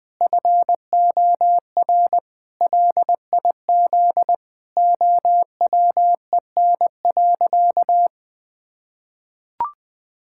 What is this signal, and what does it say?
Morse code, heard as a keyed tone, about 20 wpm, 700 Hz. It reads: FOR LIZ OWEN.
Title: Pangram Challenge v2 - 20wpm